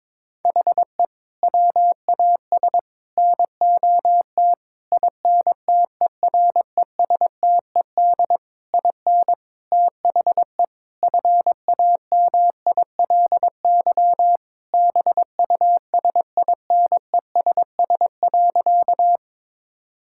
Morse code, heard as a keyed tone, about 22 wpm, 700 Hz. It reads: HE WAS NOT INTERESTED IN THE FAMILY BUSINESS.
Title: Sentences from Top 1000 Words - Review Part 6 - 22wpm